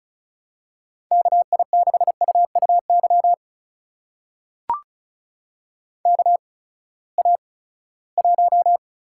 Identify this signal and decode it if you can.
Morse code, heard as a keyed tone, about 36 wpm, spaced as for 20 wpm, 700 Hz. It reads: KI6UUY KA1
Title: USA Call Signs - 8x Character Spacing - 35wpm